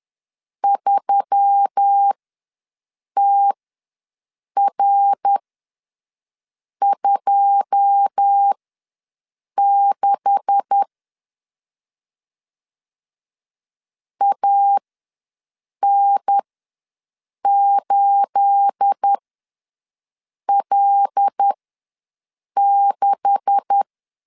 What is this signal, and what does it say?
Morse code, heard as a keyed tone, about 10 wpm, spaced as for 6 wpm, 800 Hz. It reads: 3TR26 AN8L6
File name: LEZIONI / 9 286 ws 10wpm 4032